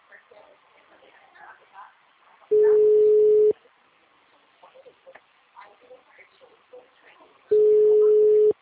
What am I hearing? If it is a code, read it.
Ringback tone (Europe, 425 Hz), with speech.